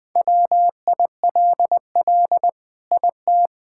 Morse code, heard as a keyed tone, about 21 wpm, 700 Hz. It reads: WILL IT
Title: Sentences from Top 1000 Words - Review Part 6 - 20wpm